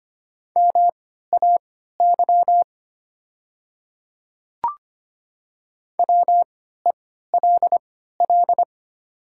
Morse code, heard as a keyed tone, about 25 wpm, spaced as for 14 wpm, 700 Hz. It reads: MAY WELL
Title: 3x Character Spacing - Top 200 Words - 25wpm